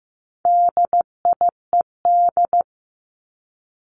Morse code, heard as a keyed tone, about 15 wpm, 700 Hz. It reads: DIED
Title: ED Suffix - Top 100 Words with Base Words 15wpm